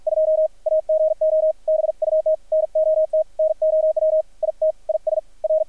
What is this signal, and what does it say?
Morse code, heard as a keyed tone, about 50 wpm, 630 Hz, with radio noise. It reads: 2 AGO BUT NOT NOW IT IS R
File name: archived / CRSnet